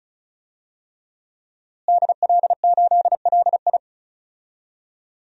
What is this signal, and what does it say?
Morse code, heard as a keyed tone, about 36 wpm, 700 Hz. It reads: DL8LI
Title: DXCC-100 Call Signs - 5 Characters - Repeated 3x - 35wpm